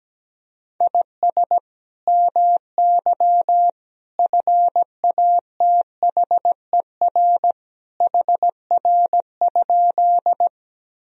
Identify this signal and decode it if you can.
Morse code, heard as a keyed tone, about 17 wpm, 700 Hz. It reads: IS MY FATHER HR?
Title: Sentences from Top 300 Words and Abbreviations 17wpm